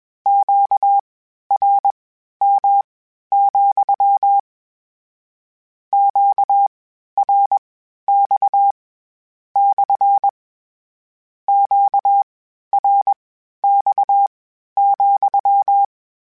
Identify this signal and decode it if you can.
Morse code, heard as a keyed tone, about 21 wpm, spaced as for 12 wpm, 800 Hz. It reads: QRM, QRX / QRX,
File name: Q Calls / Q Calls-mp3s / Q CALLS SPSL @20